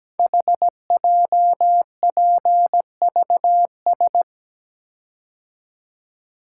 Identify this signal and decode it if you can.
Morse code, heard as a keyed tone, about 17 wpm, 700 Hz. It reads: HJPVS
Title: Code Groups - 5 Letters - 17wpm